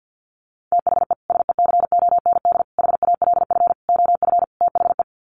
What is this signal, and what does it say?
Morse code, heard as a keyed tone, about 50 wpm, 700 Hz. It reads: THE SECOND HALF OF THE
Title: Sets of 5 Words 50wpm